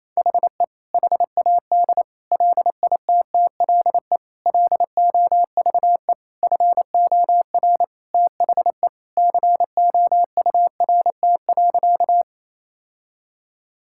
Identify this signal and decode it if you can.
Morse code, heard as a keyed tone, about 28 wpm, 700 Hz. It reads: HE HAD LITTLE LOVE FOR THE COURT.
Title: Sentences from Top 1600 Words 28wpm